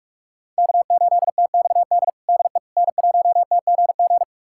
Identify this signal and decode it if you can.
Morse code, heard as a keyed tone, about 47 wpm, 700 Hz. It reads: K8TXD DE N1TGG
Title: USA Callsign DE Callsign - 5 Characters - 45wpm